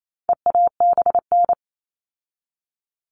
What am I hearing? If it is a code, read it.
Morse code, heard as a keyed tone, about 28 wpm, 700 Hz. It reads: EABN
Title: All Cut Numbers - 4-Digit 28wpm